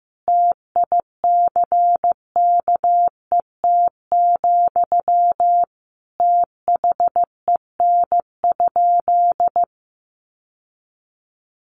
Morse code, heard as a keyed tone, about 15 wpm, 700 Hz. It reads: TICKET, THEN?
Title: Sentences from Top 1400 Words 15wpm